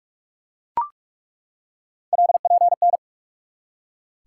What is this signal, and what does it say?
Morse code, heard as a keyed tone, about 47 wpm, 700 Hz. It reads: LPN
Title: Practice QSO - Exchange 3 - More Info - 45wpm